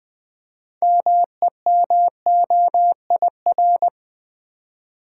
Morse code, heard as a keyed tone, about 20 wpm, 700 Hz. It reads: MEMOIR
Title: Build A Word - 3 Letters At A Time - 6-Letter Words - 20wpm